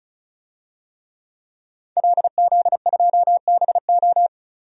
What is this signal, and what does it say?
Morse code, heard as a keyed tone, about 34 wpm, 700 Hz. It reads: LZ2BO
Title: DXCC-100 Call Signs - 5 Characters - 35wpm